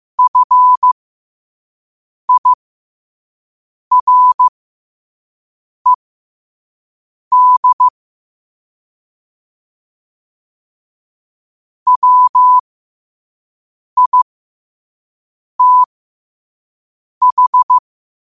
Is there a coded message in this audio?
Yes: Morse code from 0.2 to 17.8 s, heard as a keyed tone, about 15 wpm, spaced as for 5 wpm, 1000 Hz. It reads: FIRED WITH